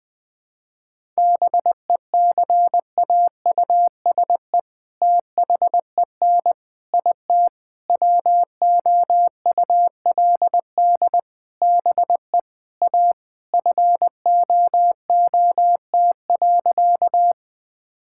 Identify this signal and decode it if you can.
Morse code, heard as a keyed tone, about 20 wpm, 700 Hz. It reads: BECAUSE THEN IT WOULD BE A FOOT.
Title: All-Age Appropriate Jokes - 20wpm